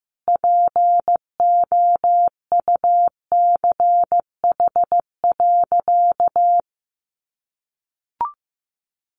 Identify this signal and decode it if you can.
Morse code, heard as a keyed tone, about 15 wpm, 700 Hz. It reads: POUCH.
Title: Easy Phrases 15wpm